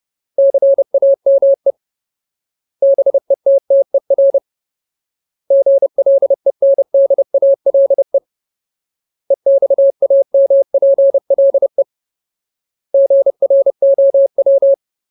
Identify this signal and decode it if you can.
Morse code, heard as a keyed tone, about 30 wpm, 550 Hz. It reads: CAME BETTER GLENDALE EXAMPLE GROW
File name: Ultimate Word List / 30WPM Ultimate